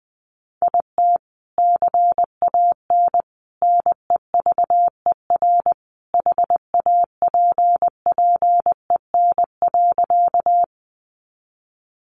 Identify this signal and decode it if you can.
Morse code, heard as a keyed tone, about 20 wpm, 700 Hz. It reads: IT CAN NEVER HAPPEN.
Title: Sentences from Top 400 Words 20wpm